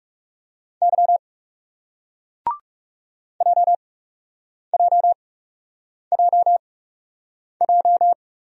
Morse code, heard as a keyed tone, about 39 wpm, spaced as for 8 wpm, 700 Hz. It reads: Y JJJJ